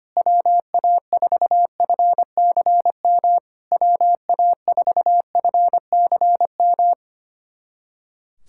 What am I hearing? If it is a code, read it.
Morse code, heard as a keyed tone, about 25 wpm, 690 Hz. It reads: WA4FCM WA4FCM
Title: SOTA QSO Practice 25wpm